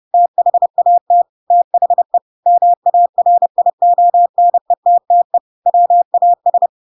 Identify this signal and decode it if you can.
Morse code, heard as a keyed tone, about 30 wpm, 700 Hz. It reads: THAT THE MARIONETTE WAS
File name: Pinocchio / Pinocchio ch7 30 30 ik1hge